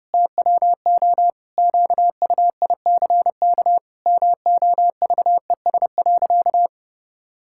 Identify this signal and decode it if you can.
Morse code, heard as a keyed tone, about 32 wpm, 700 Hz. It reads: TWO QUICK MOVES.
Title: Pangram Challenge v2 - 30wpm